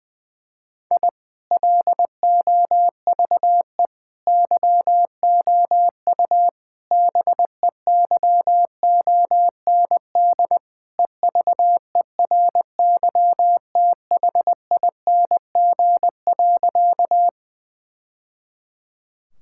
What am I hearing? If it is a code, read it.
Morse code, heard as a keyed tone, about 19 wpm, 700 Hz. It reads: I LOVE YOU BEYOND EVERYTHING.
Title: Sentences from Top 1000 Words - Review Part 6 - 20wpm